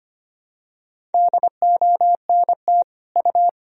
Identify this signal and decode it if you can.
Morse code, heard as a keyed tone, about 25 wpm, 700 Hz. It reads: DONT U
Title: Sentences from Top 300 Words and Abbreviations 25wpm